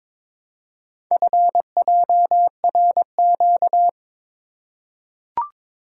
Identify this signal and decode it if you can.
Morse code, heard as a keyed tone, about 22 wpm, 690 Hz. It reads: FJRQ
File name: practice / 22wpm / Four Letter-Number Combinations - 22wpm